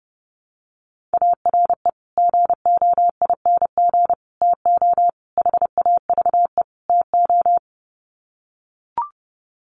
Morse code, heard as a keyed tone, about 30 wpm, 700 Hz. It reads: ARE GOING TO HAVE TO
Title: Sets of 5 Words 30wpm